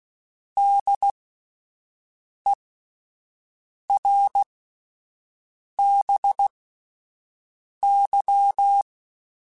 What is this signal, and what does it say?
Morse code, heard as a keyed tone, about 16 wpm, spaced as for 5 wpm, 800 Hz. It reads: DERBY
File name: fromSMs / FromSMs mp3s / Cities@5